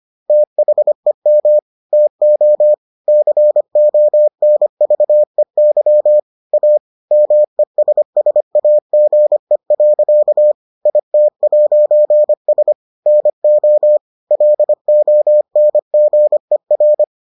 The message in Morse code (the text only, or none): THEM TO CONVEY A MESSAGE. IT'S NO LONGER